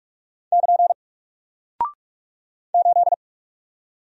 Morse code, heard as a keyed tone, about 46 wpm, 700 Hz. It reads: <KN> 8
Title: Single Character (Letter:Number:Punctuation:Prosign) 45wpm